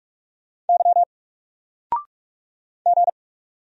Morse code, heard as a keyed tone, about 46 wpm, 700 Hz. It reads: Y G
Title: Single Letter-Number 45wpm